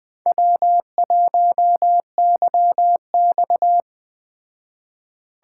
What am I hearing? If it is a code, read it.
Morse code, heard as a keyed tone, about 21 wpm, 700 Hz. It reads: W1YX
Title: Speed Racing - USA 1x2 Call Signs 17wpm